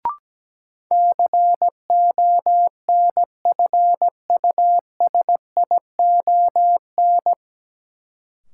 Morse code, heard as a keyed tone, about 17 wpm, 700 Hz. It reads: CONFUSION
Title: Sentences from Top 1600 Words - New Word Drill Alt - 17wpm